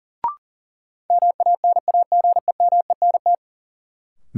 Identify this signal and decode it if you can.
Morse code, heard as a keyed tone, about 37 wpm, 700 Hz. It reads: MANAGEMENT
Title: Sentences from Top 1000 Words - Review Part 4 - New Word Drill - 40wpm